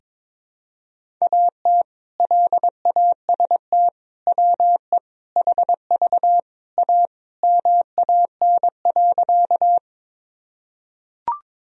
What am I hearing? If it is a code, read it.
Morse code, heard as a keyed tone, about 21 wpm, 700 Hz. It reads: AT LAST WE HV A MAN.
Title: Sentences from Top 300 Words and Abbreviations 22wpm